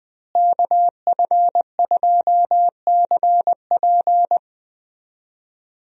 Morse code, heard as a keyed tone, about 21 wpm, 700 Hz. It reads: KF2CP
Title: USA Call Signs - 5 Characters - Repeated 3x - 20wpm